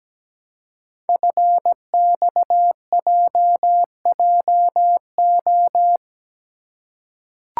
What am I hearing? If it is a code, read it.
Morse code, heard as a keyed tone, about 17 wpm, 700 Hz. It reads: FXJJO
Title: Code Groups - 5 Letters - 17wpm